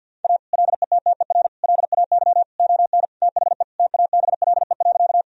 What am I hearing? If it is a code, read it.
Morse code, heard as a keyed tone, about 50 wpm, 700 Hz. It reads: A LETTER LAY ON THE TABLE.
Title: Sentences from Top 1000 Words - Review Part 2 - 50wpm